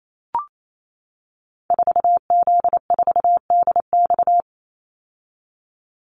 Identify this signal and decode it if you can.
Morse code, heard as a keyed tone, about 28 wpm, 700 Hz. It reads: 4Z4DX
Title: Call Signs - 28wpm